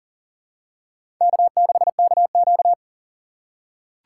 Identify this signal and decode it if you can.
Morse code, heard as a keyed tone, about 41 wpm, 690 Hz. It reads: K6KQ
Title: USA Call Signs - 4 Characters - 2x Character Spacing - 40wpm